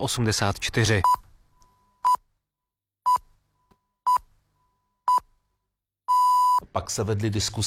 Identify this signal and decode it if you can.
Radio time signal (time pips), with speech. It is heard from 1.0 to 6.6 s.